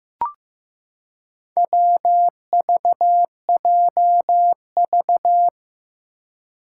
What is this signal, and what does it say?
Morse code, heard as a keyed tone, about 15 wpm, 700 Hz. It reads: WVJV
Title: Four Challenging Letter-Number Combinations 15wpm